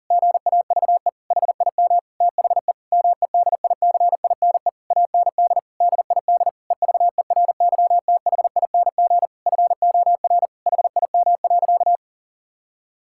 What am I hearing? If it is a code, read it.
Morse code, heard as a keyed tone, about 40 wpm, 700 Hz. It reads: GAVE HIM THE MEDICINE AND DID EVERYTHING FOR HIM.
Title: Sentences from Top 1000 Words - Review Part 6 - 40wpm